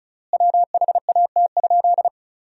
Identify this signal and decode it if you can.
Morse code, heard as a keyed tone, about 34 wpm, 700 Hz. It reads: WHAT?
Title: 3x Word-Spacing - Sentences from Top 100 Words - 35wpm